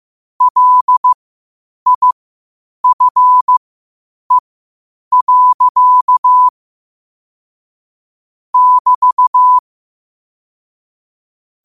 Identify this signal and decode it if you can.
Morse code, heard as a keyed tone, about 15 wpm, spaced as for 8 wpm, 1000 Hz. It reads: LIFE. <BT>
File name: qotdfiles / 8wpm